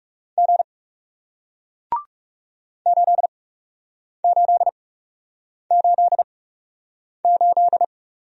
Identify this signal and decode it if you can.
Morse code, heard as a keyed tone, about 40 wpm, spaced as for 8 wpm, 700 Hz. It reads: G 8888